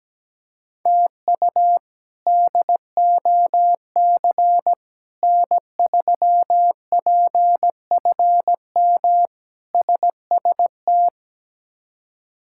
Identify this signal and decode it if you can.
Morse code, heard as a keyed tone, about 17 wpm, 700 Hz. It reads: TU DOC N3PFM SST
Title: K1USN SST QSO Practice - 17wpm